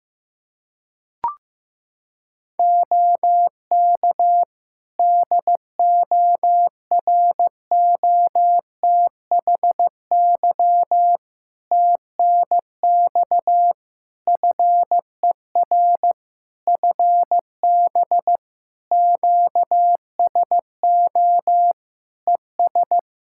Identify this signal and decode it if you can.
Morse code, heard as a keyed tone, about 15 wpm, 700 Hz. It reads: OK DOROTHY TNX FER FB QSO ES
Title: Practice QSO - Exchange 4 - Ending - 15wpm